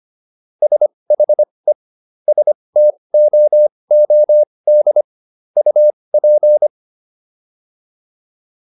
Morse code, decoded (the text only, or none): SHE STOOD UP